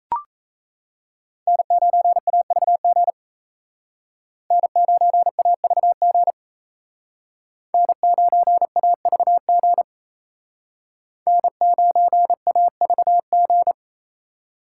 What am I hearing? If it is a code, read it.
Morse code, heard as a keyed tone, about 36 wpm, 700 Hz. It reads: N9AVG N9AVG N9AVG N9AVG